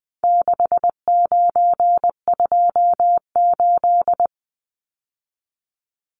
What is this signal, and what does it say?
Morse code, heard as a keyed tone, about 20 wpm, 700 Hz. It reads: 6928
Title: Numbers - 4-Digit 20wpm